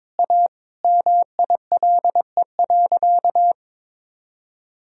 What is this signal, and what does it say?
Morse code, heard as a keyed tone, about 21 wpm, 700 Hz. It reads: A MILE.
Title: Sentences from Top 1000 Words - Review Part 4 - 22wpm